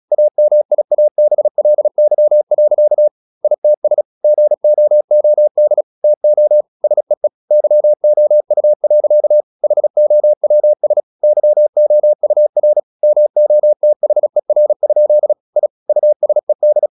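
Morse code, heard as a keyed tone, about 36 wpm, 600 Hz. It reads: AMIABLY. ITS GOOD TO SEE YOU. HOWS YOUR MOTHER? I USED
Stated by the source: Morse practice at 36 wpm, text of Ken Follett's novel Never, from CW med Gnister.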